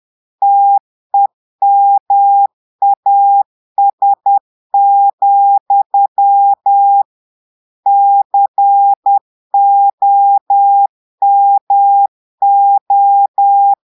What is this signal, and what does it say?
Morse code, heard as a keyed tone, about 10 wpm, 800 Hz. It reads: TEMAS, COMO
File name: Portugais - texte en clair 1 - vitesse - 10 mots minute